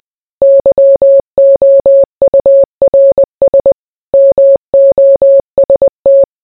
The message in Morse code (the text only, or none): YOURS MOST